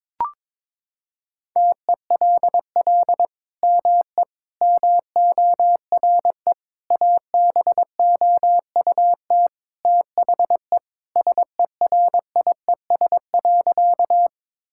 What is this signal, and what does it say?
Morse code, heard as a keyed tone, about 22 wpm, 700 Hz. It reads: TELL ME MORE ABOUT THE SERIES.
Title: Sentences from Top 1000 Words - Review Part 5 - 22wpm